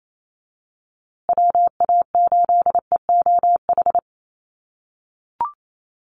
Morse code, heard as a keyed tone, about 28 wpm, 700 Hz. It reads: WA8EOH
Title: USA Call Signs Repeated 2x 28wpm